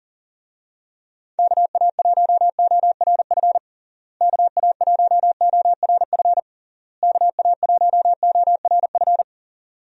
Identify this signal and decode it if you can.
Morse code, heard as a keyed tone, about 40 wpm, 700 Hz. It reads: KA1ORF KA1ORF KA1ORF
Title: USA Call Signs - 6 Characters - Repeated 3x - 40wpm